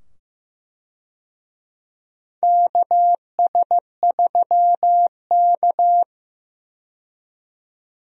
Morse code, heard as a keyed tone, about 15 wpm, 700 Hz. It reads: KS3K
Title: USA Call Signs - 4 Characters - 15wpm